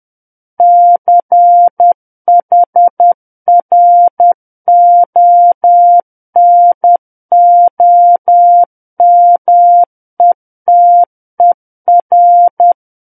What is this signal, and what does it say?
Morse code, heard as a keyed tone, about 10 wpm, 700 Hz. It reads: CHRONOMETER
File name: Book One / 01-Chapter 1 - 10 WPM